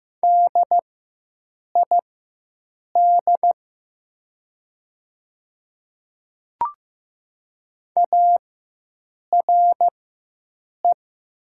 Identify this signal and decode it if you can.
Morse code, heard as a keyed tone, about 15 wpm, spaced as for 7 wpm, 700 Hz. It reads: DID ARE
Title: 4x Character Spacing - Top 200 Words - 15wpm